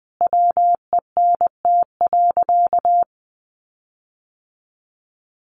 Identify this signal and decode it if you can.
Morse code, heard as a keyed tone, about 20 wpm, 700 Hz. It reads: WENT.
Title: Sentences from Top 1500 Words 20wpm